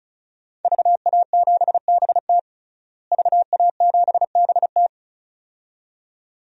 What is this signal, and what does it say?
Morse code, heard as a keyed tone, about 36 wpm, 700 Hz. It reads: VA7BT VA7BT
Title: DXCC-100 Call Signs - 5 Characters - Repeated 2x - 35wpm